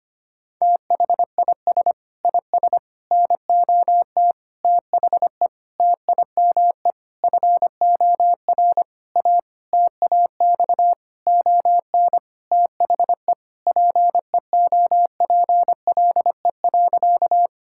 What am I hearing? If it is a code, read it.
Morse code, heard as a keyed tone, about 25 wpm, 700 Hz. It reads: THIS IS NOT THE TIME FOR A TAX ON THE PEOPLE.
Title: Sentences from Top 1000 Words - Review Part 5 - 25wpm